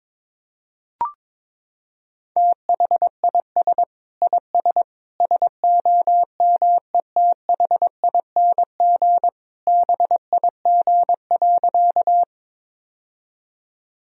Morse code, heard as a keyed tone, about 22 wpm, 700 Hz. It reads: THIS IS SOMETHING BIG.
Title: Sentences from Top 1000 Words - Review Part 1 - 22wpm